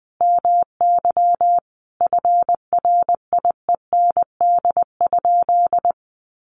Morse code, heard as a keyed tone, about 20 wpm, 700 Hz. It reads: MY FRIEND?
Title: Sentences from Top 1400 Words 20wpm